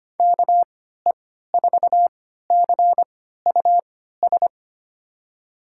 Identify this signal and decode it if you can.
Morse code, heard as a keyed tone, about 25 wpm, spaced as for 14 wpm, 700 Hz. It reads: KE4CUS
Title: USA Call Signs - 6 Characters - 3x Character Spacing - 25wpm